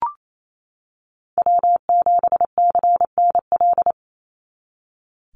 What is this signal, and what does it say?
Morse code, heard as a keyed tone, about 28 wpm, 700 Hz. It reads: W7CNL
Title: NA SOTA Chaser Callsigns 28wpm